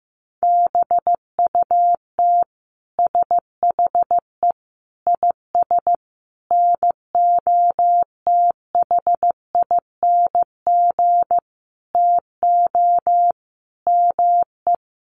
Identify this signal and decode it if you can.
Morse code, heard as a keyed tone, about 15 wpm, 700 Hz. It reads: BUT SHE IS NOTHING TO ME
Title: Sentences from Top 500 Words 15wpm